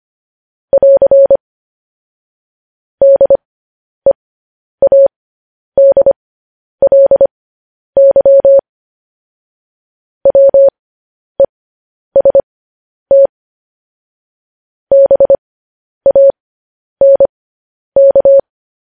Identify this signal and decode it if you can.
Morse code, heard as a keyed tone, about 25 wpm, spaced as for 10 wpm, 550 Hz. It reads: <AR> DEADLY WEST BANK